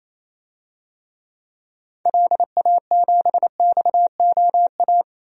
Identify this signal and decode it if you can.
Morse code, heard as a keyed tone, about 28 wpm, 700 Hz. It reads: LA7XOA
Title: DXCC-100 Call Signs - 6 Characters - 28wpm